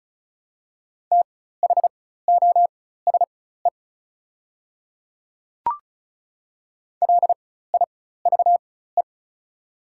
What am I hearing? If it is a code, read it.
Morse code, heard as a keyed tone, about 34 wpm, spaced as for 16 wpm, 700 Hz. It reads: THOSE LIVE
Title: 4x Character Spacing - Top 300 Words - 35wpm